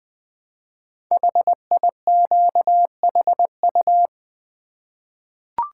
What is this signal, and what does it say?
Morse code, heard as a keyed tone, about 20 wpm, 700 Hz. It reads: HIQHU
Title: Code Groups - 5 Letters - 20wpm